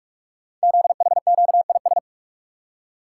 Morse code, heard as a keyed tone, about 44 wpm, 700 Hz. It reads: ZHQIS
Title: Code Groups - 5 Letters - 45wpm